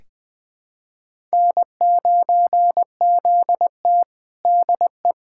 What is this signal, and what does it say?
Morse code, heard as a keyed tone, about 20 wpm, 700 Hz. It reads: N9ZT DE
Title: USA Callsign DE Callsign - 4 Characters - 20wpm